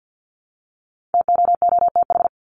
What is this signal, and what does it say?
Morse code, heard as a keyed tone, about 50 wpm, 700 Hz. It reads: TOOTH
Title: Top 1751-2000 Words 50wpm